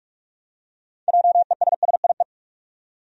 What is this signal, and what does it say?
Morse code, heard as a keyed tone, about 46 wpm, 700 Hz. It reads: JESSIE
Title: USA Common Names - 45wpm